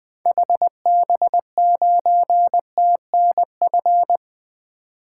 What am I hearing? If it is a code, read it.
Morse code, heard as a keyed tone, about 20 wpm, 700 Hz. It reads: HB9TNF
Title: 2023 SOTA Activator Call Signs - 20wpm